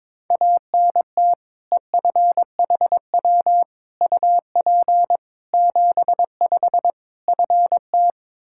Morse code, heard as a keyed tone, about 22 wpm, 700 Hz. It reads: ANT EFHW UP 75 FT
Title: Practice QSO - Exchange 2 - Info - 22wpm